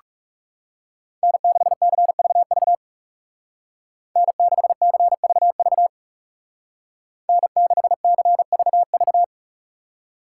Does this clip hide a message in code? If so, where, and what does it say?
Yes: Morse code from 1.2 to 9.2 s, heard as a keyed tone, about 40 wpm, 700 Hz. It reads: N6CVV N6CVV N6CVV